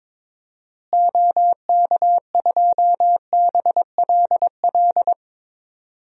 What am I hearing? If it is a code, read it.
Morse code, heard as a keyed tone, about 22 wpm, 700 Hz. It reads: OK2BLL
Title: DXCC-100 Call Signs - 6 Characters - Repeated 3x - 22wpm